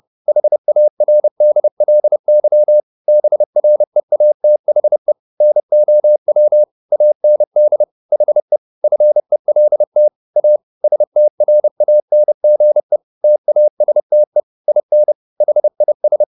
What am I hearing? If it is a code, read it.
Morse code, heard as a keyed tone, about 30 wpm, 600 Hz. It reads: HARDLY BREATHE NOW AND HE FELT A STRANGE TASTE IN HIS